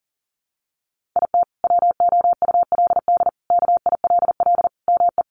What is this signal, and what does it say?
Morse code, heard as a keyed tone, about 40 wpm, 700 Hz. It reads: IT WOULD KILL ME
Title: Sentences from Top 1400 Words 40wpm